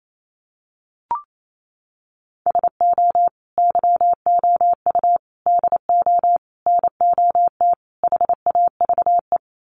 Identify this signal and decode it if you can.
Morse code, heard as a keyed tone, about 28 wpm, 700 Hz. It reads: SO YOU DO NOT HAVE
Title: Sets of 5 Words Encore 28wpm